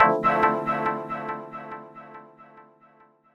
There is no code.